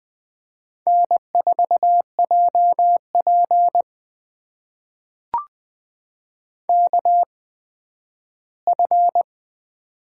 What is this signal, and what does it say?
Morse code, heard as a keyed tone, about 20 wpm, spaced as for 12 wpm, 700 Hz. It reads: N4JP KF
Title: USA Call Signs - 4 Characters - 8x Character Spacing - 20wpm